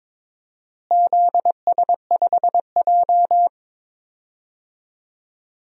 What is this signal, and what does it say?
Morse code, heard as a keyed tone, about 22 wpm, 700 Hz. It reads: ZS5J
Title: DXCC-100 Call Signs - 4 Characters - Repeated 3x - 22wpm